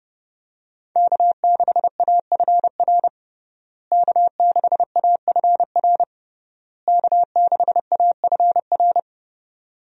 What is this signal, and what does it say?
Morse code, heard as a keyed tone, about 30 wpm, 700 Hz. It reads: K6AFR K6AFR K6AFR
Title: USA Call Signs - 5 Characters - Repeated 3x - 30wpm